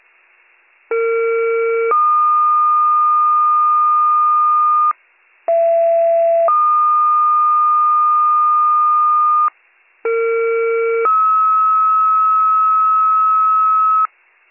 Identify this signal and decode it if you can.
Two-tone page, heard as a one-second tone, then a three-second tone, about 460 then 1220 Hz, about 670 then 1220 Hz, about 460 then 1290 Hz. It is heard from 0.9 to 14.1 s.